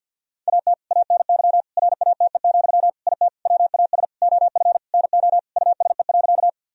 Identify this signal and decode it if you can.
Morse code, heard as a keyed tone, about 50 wpm, 700 Hz. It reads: AT ANY RATE, IT WAS OF NO USE.
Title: Sentences from Top 1000 Words - Review Part 4 - 50wpm